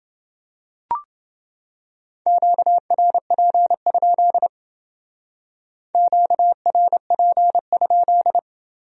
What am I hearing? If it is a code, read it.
Morse code, heard as a keyed tone, about 28 wpm, 700 Hz. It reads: QRP? QRP?